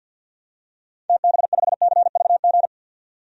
Morse code, heard as a keyed tone, about 50 wpm, 700 Hz. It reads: TB5CVG
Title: DXCC-100 Call Signs - 6 Characters - 50wpm